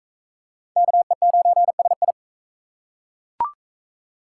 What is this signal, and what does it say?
Morse code, heard as a keyed tone, about 38 wpm, 700 Hz. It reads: KE9SI